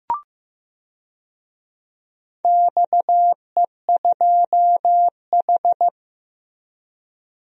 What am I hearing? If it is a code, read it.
Morse code, heard as a keyed tone, about 15 wpm, 700 Hz. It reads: XE2H
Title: DXCC-100 Call Signs - 4 Characters - Repeated 2x - 15wpm